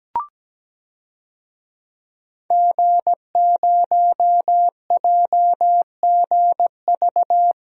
Morse code, heard as a keyed tone, about 17 wpm, 700 Hz. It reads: G0JGV
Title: DXCC-100 Call Signs - 5 Characters - Repeated 3x - 17wpm